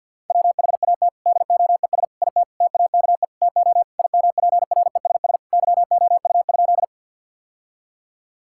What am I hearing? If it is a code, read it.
Morse code, heard as a keyed tone, about 50 wpm, 700 Hz. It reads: WHAT DOES IT TAKE TO IMPRESS YOU?
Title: Sentences from Top 1600 Words 50wpm